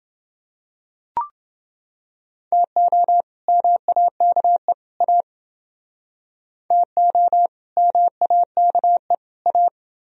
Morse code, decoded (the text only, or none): TO MAKE A TO MAKE A